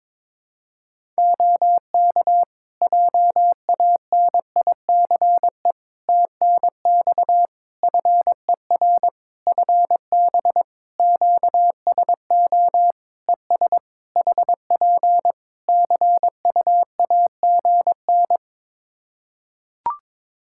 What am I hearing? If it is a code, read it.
Morse code, heard as a keyed tone, about 22 wpm, 700 Hz. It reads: OK JANICE TNX FER FB QSO ES HP CUAGN
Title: K1USN SST QSO Practice 22wpm